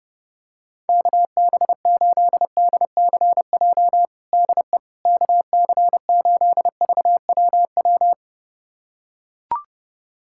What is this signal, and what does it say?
Morse code, heard as a keyed tone, about 32 wpm, 700 Hz. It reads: KB8DCJ DE KC8VWW
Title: USA Callsign DE Callsign - 6 Characters - 30wpm